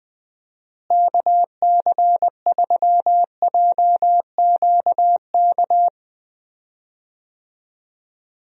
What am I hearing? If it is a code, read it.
Morse code, heard as a keyed tone, about 20 wpm, 700 Hz. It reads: KC3JQK
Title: USA Call Signs - 6 Characters - Repeated 3x - 20wpm